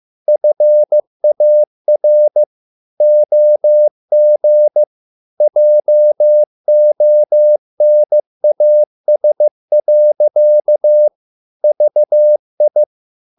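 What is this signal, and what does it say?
Morse code, heard as a keyed tone, about 15 wpm, 600 Hz. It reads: FAR OG JONAS. VI
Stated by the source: Morse code at a slow 15 wpm from CW med Gnister.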